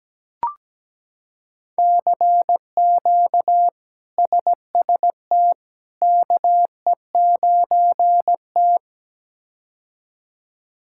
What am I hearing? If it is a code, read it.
Morse code, heard as a keyed tone, about 17 wpm, 700 Hz. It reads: CQ SST KE9T
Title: K1USN SST QSO Practice - 17wpm